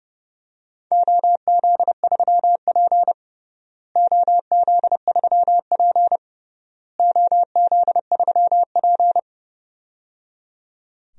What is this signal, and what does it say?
Morse code, heard as a keyed tone, about 29 wpm, 700 Hz. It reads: OZ3P OZ3P OZ3P
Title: DXCC-100 Call Signs - 4 Characters - Repeated 3x - 30wpm